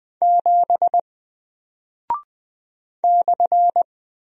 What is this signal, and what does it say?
Morse code, heard as a keyed tone, about 20 wpm, 700 Hz. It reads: 7 /
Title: Single Character (Letter:Number:Punctuation:Prosign) Alt 20wpm